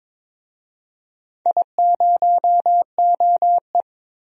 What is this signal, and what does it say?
Morse code, heard as a keyed tone, about 22 wpm, 700 Hz. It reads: I0OE
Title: DXCC-100 Call Signs - 4 Characters - Repeated 2x - 22wpm